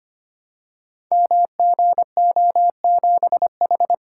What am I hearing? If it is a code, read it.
Morse code, heard as a keyed tone, about 25 wpm, 700 Hz. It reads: MGO7H
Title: Code Groups - 5 Letter:Numbers - 25wpm